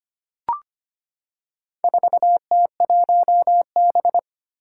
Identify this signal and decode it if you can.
Morse code, heard as a keyed tone, about 25 wpm, 700 Hz. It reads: 4T1B
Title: Four Challenging Letter-Number Combinations 25wpm